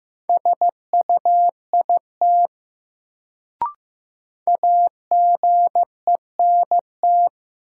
Morse code, heard as a keyed tone, about 15 wpm, 700 Hz. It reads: SUIT AGENT
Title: Sentences from Top 1600 Words - New Word Drill Alt - 15wpm